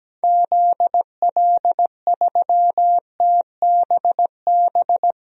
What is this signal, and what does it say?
Morse code, heard as a keyed tone, about 17 wpm, 700 Hz. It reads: ZL3TBB
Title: DXCC-100 Call Signs - 6 Characters - 17wpm